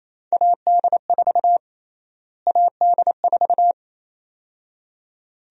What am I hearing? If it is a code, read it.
Morse code, heard as a keyed tone, about 28 wpm, 700 Hz. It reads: AD4 AD4
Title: USA Call Sign Prefixes - Repeated 2x - 28wpm